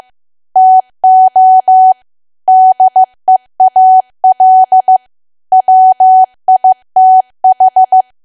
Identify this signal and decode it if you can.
Morse code, heard as a keyed tone, about 15 wpm, 750 Hz. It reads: TO DEAL WITH
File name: cwa / W1AW-Level2 / 15WPM